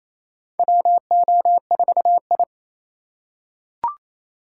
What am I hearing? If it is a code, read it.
Morse code, heard as a keyed tone, about 27 wpm, 700 Hz. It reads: WO4I
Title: USA Call Signs - 4 Characters - 4x Character Spacing - 28wpm